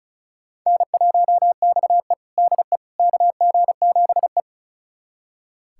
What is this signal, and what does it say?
Morse code, heard as a keyed tone, about 36 wpm, 700 Hz. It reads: N1XE DE KG7E
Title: USA Callsign DE Callsign - 4 Characters - 35wpm